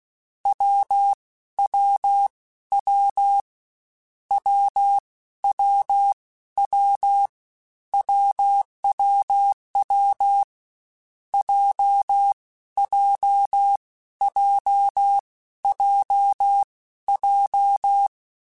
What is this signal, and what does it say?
Morse code, heard as a keyed tone, about 16 wpm, spaced as for 12 wpm, 800 Hz. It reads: WWW WWW WWW JJJJJ
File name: Difficult Combos / Difficult Combos-mp3s / AWJ1 @15